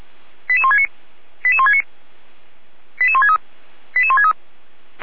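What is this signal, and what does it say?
Five-tone selective call (ZVEI), heard as quick runs of five tones, tones of about 70 ms. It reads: 88169 88169 88163 88163
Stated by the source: ZVEI five-tone call 88352 received on an SDR.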